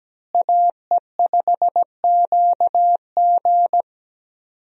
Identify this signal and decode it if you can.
Morse code, heard as a keyed tone, about 17 wpm, 700 Hz. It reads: AE5QG
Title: USA Call Signs - 5 Characters - Repeated 3x - 17wpm